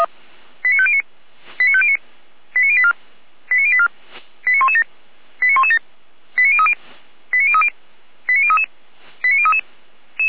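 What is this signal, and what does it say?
Five-tone selective call (ZVEI), heard as quick runs of five tones, tones of about 70 ms. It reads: 89509 89509 89084 89084 89117 89117 89030 89030 89033 89033
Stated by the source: ZVEI five-tone selective call 89115 received by SDR.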